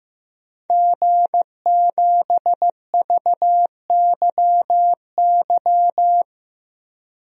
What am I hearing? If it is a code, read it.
Morse code, heard as a keyed tone, about 15 wpm, 700 Hz. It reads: G7VYY
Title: DXCC-100 Call Signs - 5 Characters - Repeated 2x - 15wpm